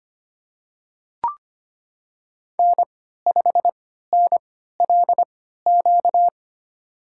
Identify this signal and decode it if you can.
Morse code, heard as a keyed tone, about 25 wpm, spaced as for 14 wpm, 700 Hz. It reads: N5NLQ
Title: USA Call Signs - 5 Characters - 3x Character Spacing - 25wpm